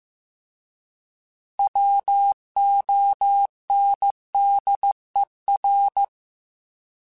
Morse code, heard as a keyed tone, about 15 wpm, 800 Hz. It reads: WONDER